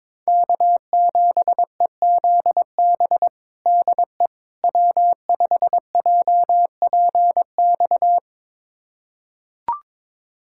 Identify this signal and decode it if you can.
Morse code, heard as a keyed tone, about 22 wpm, 700 Hz. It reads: K7EZB DE W5JPX
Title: USA Callsign DE Callsign - 5 Characters - 22wpm